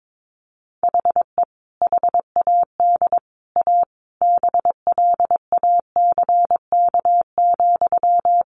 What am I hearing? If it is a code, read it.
Morse code, heard as a keyed tone, about 22 wpm, 700 Hz. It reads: HE HAD A BLACK,
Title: Sentences from Top 1500 Words 22wpm